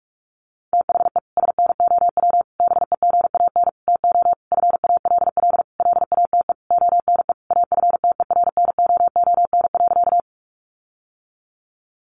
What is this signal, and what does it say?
Morse code, heard as a keyed tone, about 44 wpm, 700 Hz. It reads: THE SNOW BEGAN TO FALL LATE ONE AFTERNOON.
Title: Sentences from Top 1500 Words 45wpm